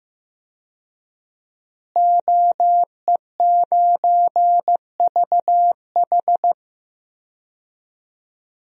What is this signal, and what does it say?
Morse code, heard as a keyed tone, about 14 wpm, 700 Hz. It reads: OE9VH
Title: DXCC-100 Call Signs - 5 Characters - Repeated 3x - 15wpm